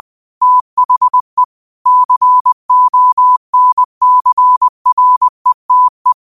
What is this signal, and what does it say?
Morse code, heard as a keyed tone, about 20 wpm, 1000 Hz. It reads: THE CONCRETE